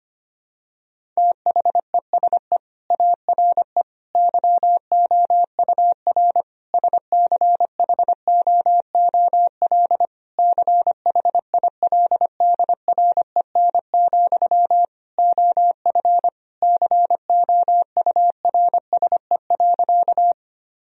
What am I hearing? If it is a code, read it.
Morse code, heard as a keyed tone, about 25 wpm, 700 Hz. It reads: THESE ARE YOUR SCHOOL CHILDREN, OF COURSE.
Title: Sentences from Top 1000 Words - Review Part 1 - 25wpm